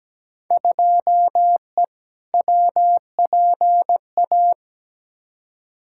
Morse code, heard as a keyed tone, about 17 wpm, 700 Hz. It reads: 2E WPA
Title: Field Day Exchanges Encore - 17wpm